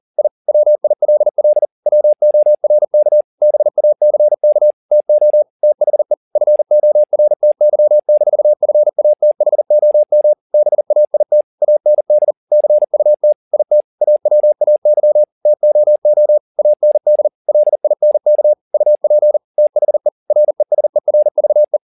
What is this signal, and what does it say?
Morse code, heard as a keyed tone, about 40 wpm, 600 Hz. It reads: I WILL WORK BACK TO THE FORTY-FATHOM BAIT AND CUT IT AWAY TOO AND LINK UP THE RESERVE